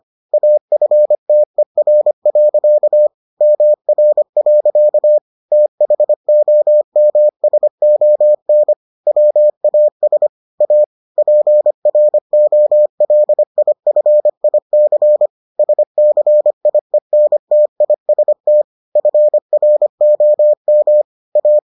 Morse code, heard as a keyed tone, about 25 wpm, 600 Hz. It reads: AFTER. MR. THOMSON WAS A PROLIFIC SCIENTIST FROM A